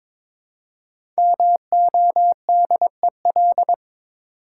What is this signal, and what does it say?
Morse code, heard as a keyed tone, about 23 wpm, 700 Hz. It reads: MODEL
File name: 5 Letter Words - First 3 Letters Given - 22wpm